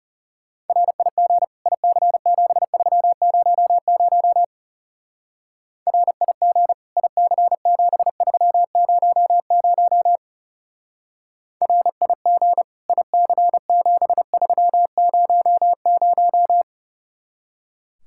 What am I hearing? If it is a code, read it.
Morse code, heard as a keyed tone, about 34 wpm, 700 Hz. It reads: RIG IC7300 RIG IC7300 RIG IC7300
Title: Speed-Racing Common QSO Phrases - 30wpm